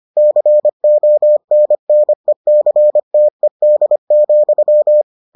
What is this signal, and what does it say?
Morse code, heard as a keyed tone, about 25 wpm, 600 Hz. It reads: CONNECTED,